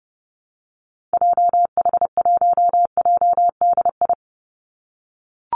Morse code, heard as a keyed tone, about 30 wpm, 700 Hz. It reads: JH1JDI
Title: Call Signs - 30wpm